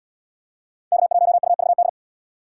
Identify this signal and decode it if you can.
Morse code, heard as a keyed tone, about 70 wpm, 700 Hz. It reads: K1UVD